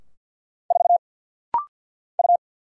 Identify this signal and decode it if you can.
Morse code, heard as a keyed tone, about 49 wpm, 700 Hz. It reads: 4 U
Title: Single Letter-Number Alt - Rapid-Fire 50wpm